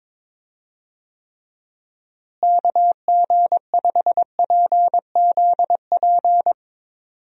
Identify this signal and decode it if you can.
Morse code, heard as a keyed tone, about 23 wpm, 700 Hz. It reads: KG5PZP
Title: USA Call Signs - 6 Characters - Repeated 3x - 22wpm